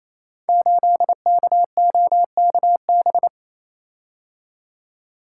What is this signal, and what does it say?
Morse code, heard as a keyed tone, about 28 wpm, 700 Hz. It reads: 8KOKB
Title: Code Groups - 5 Letter:Numbers - 28wpm